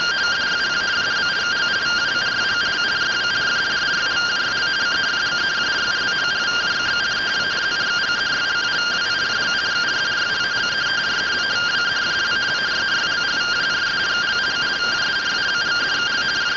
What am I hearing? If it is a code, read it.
Two-tone teleprinter signal, about 170 Hz shift.
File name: findings / CT AXIAL -111.gen-44 rtty rx